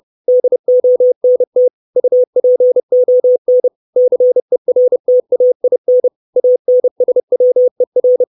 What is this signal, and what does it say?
Morse code, heard as a keyed tone, about 30 wpm, 500 Hz. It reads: DONT UPON CERTAIN ANSWER
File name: 632 Words 30WPM